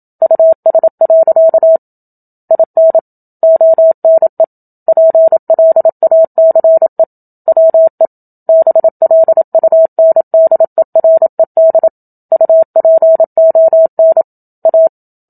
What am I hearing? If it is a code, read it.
Morse code, heard as a keyed tone, about 27 wpm, 650 Hz. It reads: US. IN ONE PLACE WE BLUNDERED UPON A
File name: Book Two / 01-Chapter 1 - 27 WPM